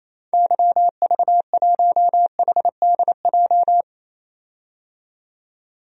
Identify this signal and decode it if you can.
Morse code, heard as a keyed tone, about 27 wpm, 700 Hz. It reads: YV1HDJ
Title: DXCC-100 Call Signs - 6 Characters - 28wpm